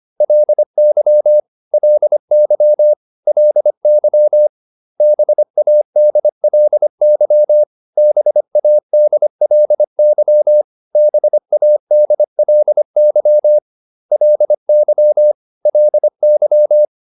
Morse code, heard as a keyed tone, about 25 wpm, 600 Hz. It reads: LY LY LY BADLY BADLY BADLY LY LY